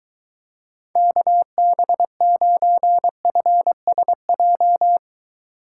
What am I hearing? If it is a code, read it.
Morse code, heard as a keyed tone, about 23 wpm, 700 Hz. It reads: KB9FSJ